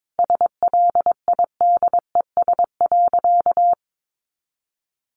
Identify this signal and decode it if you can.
Morse code, heard as a keyed tone, about 22 wpm, 700 Hz. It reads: SLIDES.
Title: Easy Phrases 22wpm